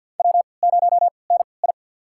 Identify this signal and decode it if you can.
Morse code, heard as a keyed tone, about 46 wpm, spaced as for 27 wpm, 700 Hz. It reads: W0NI
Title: USA Call Signs - 3x Character Spacing - 50wpm